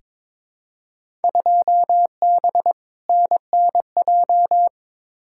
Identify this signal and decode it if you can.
Morse code, heard as a keyed tone, about 21 wpm, 700 Hz. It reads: 2B NNJ
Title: Field Day Exchanges Encore - 22wpm